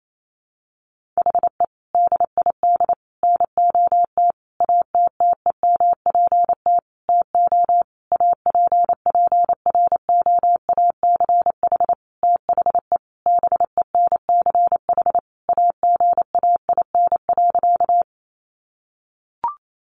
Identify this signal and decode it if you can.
Morse code, heard as a keyed tone, about 28 wpm, 700 Hz. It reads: HE DID NOT ATTEMPT TO APPROACH THE BENCH AGAIN.
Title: Sentences from Top 1400 Words 28wpm